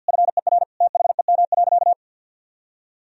Morse code, heard as a keyed tone, about 49 wpm, 700 Hz. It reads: FER THEM.
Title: Sentences from Top 300 Words and Abbreviations 50wpm